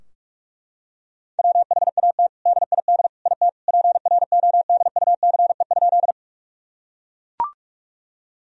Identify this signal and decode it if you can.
Morse code, heard as a keyed tone, about 41 wpm, 700 Hz. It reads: WHAT DID IT PRODUCE?
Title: Sentences from Top 1000 Words - Review Part 3 - 45wpm